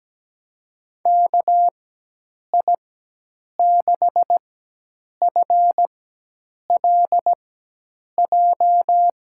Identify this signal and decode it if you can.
Morse code, heard as a keyed tone, about 17 wpm, spaced as for 8 wpm, 700 Hz. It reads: KI6FLJ
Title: USA Call Signs - 6 Characters - 4x Character Spacing - 17wpm